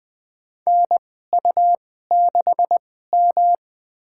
Morse code, heard as a keyed tone, about 20 wpm, spaced as for 15 wpm, 700 Hz. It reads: NU6M
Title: USA Call Signs - 2x Character Spacing - 20wpm